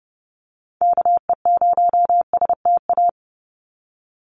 Morse code, heard as a keyed tone, about 30 wpm, 700 Hz. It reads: KE0STA
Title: USA Call Signs 30wpm